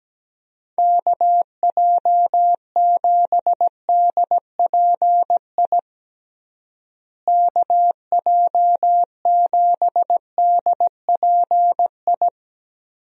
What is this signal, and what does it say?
Morse code, heard as a keyed tone, about 17 wpm, 700 Hz. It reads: KJ7DPI KJ7DPI
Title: USA Call Signs - 6 Characters - Repeated 3x - 17wpm